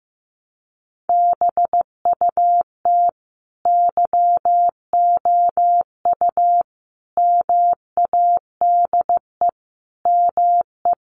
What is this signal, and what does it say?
Morse code, heard as a keyed tone, about 15 wpm, 700 Hz. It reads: BUT YOU MADE ME
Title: Sentences from Top 1300 Words 15wpm